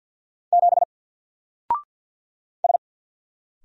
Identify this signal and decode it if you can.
Morse code, heard as a keyed tone, about 48 wpm, 700 Hz. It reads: 7 S
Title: Single Letter-Number Alt 50wpm